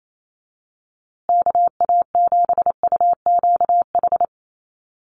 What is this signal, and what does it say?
Morse code, heard as a keyed tone, about 28 wpm, 700 Hz. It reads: KA7UQH
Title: USA Call Signs Repeated 2x 28wpm